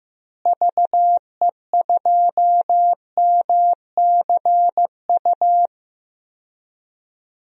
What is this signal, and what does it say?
Morse code, heard as a keyed tone, about 15 wpm, 700 Hz. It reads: VE2MCU
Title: DXCC-100 Call Signs - 6 Characters - Repeated 3x - 15wpm